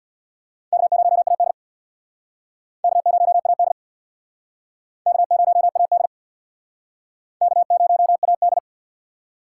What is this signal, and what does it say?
Morse code, heard as a keyed tone, about 67 wpm, 700 Hz. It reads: K0AD K0AD K0AD K0AD